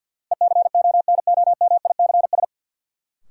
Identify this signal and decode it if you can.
Morse code, heard as a keyed tone, about 46 wpm, 700 Hz. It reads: ECONOMICS
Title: Sentences from Top 1000 Words - Review Part 3 - New Word Drill - 50wpm